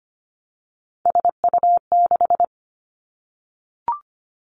Morse code, heard as a keyed tone, about 25 wpm, 700 Hz. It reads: SU6
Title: Three Challenging Letter-Number Combinations 25wpm